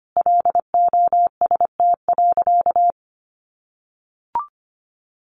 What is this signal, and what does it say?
Morse code, heard as a keyed tone, about 25 wpm, 700 Hz. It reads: LOST.
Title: Additional Easy Phrases 25wpm